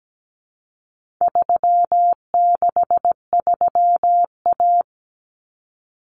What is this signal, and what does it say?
Morse code, heard as a keyed tone, about 17 wpm, 700 Hz. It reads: 363A
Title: Common Cut Numbers - 4-Digit 17wpm